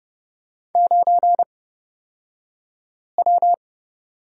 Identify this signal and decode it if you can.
Morse code, heard as a keyed tone, about 30 wpm, spaced as for 5 wpm, 700 Hz. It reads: 9W
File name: Lesson 38d - Letter q - Two Characters 30wpm